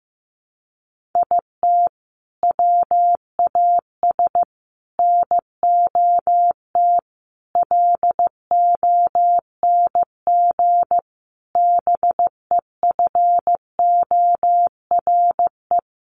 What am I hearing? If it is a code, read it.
Morse code, heard as a keyed tone, about 15 wpm, 700 Hz. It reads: IT WAS NOT LONG BEFORE
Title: Sets of 5 Words Encore2 15wpm